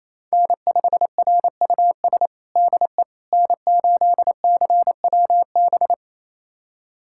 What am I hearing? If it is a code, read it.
Morse code, heard as a keyed tone, about 27 wpm, 700 Hz. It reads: N5RUS DE N8CWB
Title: USA Callsign DE Callsign - 5 Characters - 28wpm